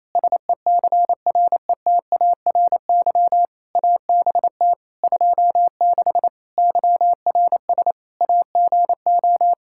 Morse code, heard as a keyed tone, about 28 wpm, 700 Hz. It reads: SECRETARY ABT 26 YRS AGO
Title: K1USN SST QSO Practice 28wpm